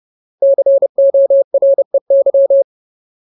Morse code, heard as a keyed tone, about 30 wpm, 550 Hz. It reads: COREY
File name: Ultimate Word List / 30WPM Ultimate